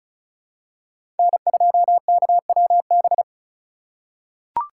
Morse code, heard as a keyed tone, about 36 wpm, 700 Hz. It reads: N2KWB
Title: USA Call Signs - 5 Characters - 2x Character Spacing - 35wpm